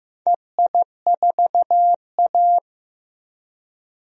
Morse code, heard as a keyed tone, about 15 wpm, 700 Hz. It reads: EI4A
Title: DXCC-100 Call Signs - 4 Characters - Repeated 2x - 15wpm